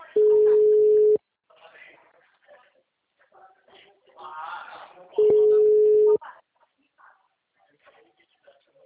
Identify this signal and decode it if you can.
Ringback tone (Europe, 425 Hz), with speech.